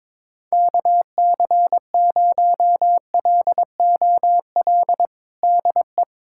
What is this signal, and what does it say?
Morse code, heard as a keyed tone, about 22 wpm, 700 Hz. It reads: KC0LOL DE
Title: USA Callsign DE Callsign 22wpm